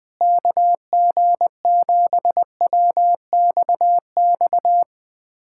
Morse code, heard as a keyed tone, about 21 wpm, 700 Hz. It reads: KG7WXX
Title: USA Callsign DE Callsign 20wpm